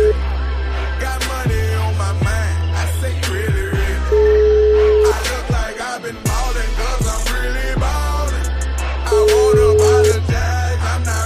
Ringback tone (Europe, 425 Hz), with noise.